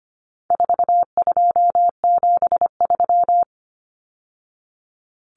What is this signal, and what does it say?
Morse code, heard as a keyed tone, about 25 wpm, 700 Hz. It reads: 4273
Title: Numbers - 4-Digit 25wpm